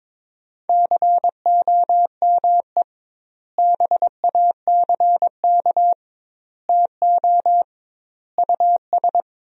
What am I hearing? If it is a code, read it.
Morse code, heard as a keyed tone, about 22 wpm, 700 Hz. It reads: COME BACK TO US
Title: 2x Word-Spacing - Sentences from Top 500 Words - 22wpm